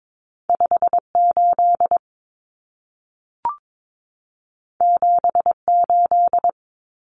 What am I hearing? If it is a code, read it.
Morse code, heard as a keyed tone, about 22 wpm, 700 Hz. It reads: 58 78
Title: Common Cut Numbers - 2-Digit 22wpm